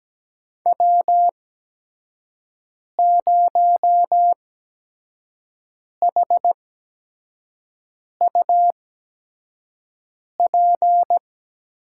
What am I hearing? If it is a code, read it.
Morse code, heard as a keyed tone, about 17 wpm, spaced as for 5 wpm, 700 Hz. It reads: W0HUP
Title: USA Call Signs - 5 Characters - 8x Character Spacing - 17wpm